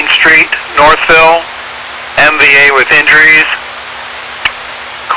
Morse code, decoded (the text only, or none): none